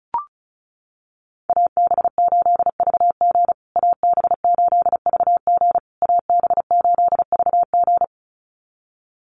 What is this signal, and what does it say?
Morse code, heard as a keyed tone, about 34 wpm, 700 Hz. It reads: AB8VG AB8VG AB8VG
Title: USA Call Signs Repeated 3x 35wpm